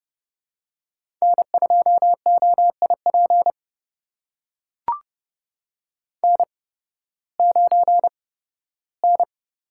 Morse code, heard as a keyed tone, about 30 wpm, spaced as for 16 wpm, 700 Hz. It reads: N2OIP N9N
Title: USA Call Signs - 5 Characters - 8x Character Spacing - 30wpm